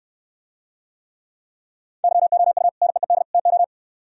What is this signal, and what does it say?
Morse code, heard as a keyed tone, about 69 wpm, 700 Hz. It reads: YOU NEED TO